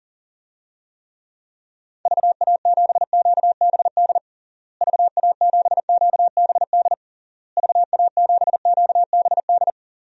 Morse code, heard as a keyed tone, about 40 wpm, 700 Hz. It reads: VA7QBD VA7QBD VA7QBD
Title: DXCC-100 Call Signs - 6 Characters - Repeated 3x - 40wpm